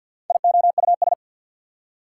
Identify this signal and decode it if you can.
Morse code, heard as a keyed tone, about 48 wpm, 700 Hz. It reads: IOUS
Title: Prefixes and Suffixes 50wpm